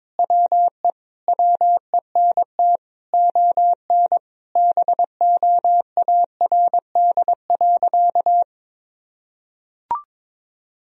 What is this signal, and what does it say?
Morse code, heard as a keyed tone, about 22 wpm, 700 Hz. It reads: WE WENT ON BOARD.
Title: Sentences from Top 1000 Words - Review Part 4 - 22wpm